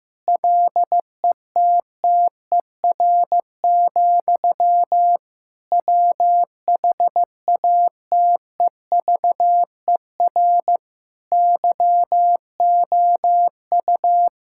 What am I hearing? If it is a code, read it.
Morse code, heard as a keyed tone, about 15 wpm, 700 Hz. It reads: LETTER, WHATEVER YOU
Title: Sentences from Top 1600 Words 15wpm